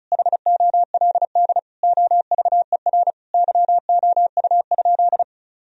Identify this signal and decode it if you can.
Morse code, heard as a keyed tone, about 34 wpm, 700 Hz. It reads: HOLD OVER YOU?
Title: Sentences from Top 1000 Words - Review Part 2 - 35wpm